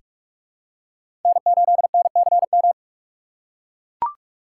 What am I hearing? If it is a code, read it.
Morse code, heard as a keyed tone, about 42 wpm, 700 Hz. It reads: N8NCM
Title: USA Call Signs - 5 Characters - 2x Character Spacing - 45wpm